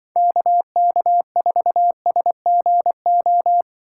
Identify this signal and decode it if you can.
Morse code, heard as a keyed tone, about 23 wpm, 700 Hz. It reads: KK4SGO
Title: Speed Racing - USA 2x3 Call Signs 20wpm